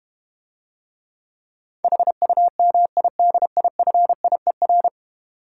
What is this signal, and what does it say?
Morse code, heard as a keyed tone, about 32 wpm, 700 Hz. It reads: HUMIDIFIER